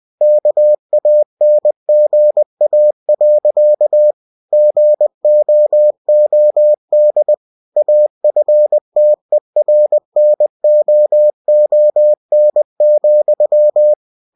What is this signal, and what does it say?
Morse code, heard as a keyed tone, about 20 wpm, 600 Hz. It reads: KANGA. GOOD AFTERNOON,
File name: cw-books / winnie-the-pooh / winnie-20wpm-page-0066